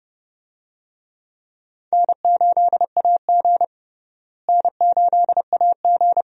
Morse code, heard as a keyed tone, about 30 wpm, 700 Hz. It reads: N8AG N8AG
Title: USA Call Signs - 4 Characters - Repeated 2x - 30wpm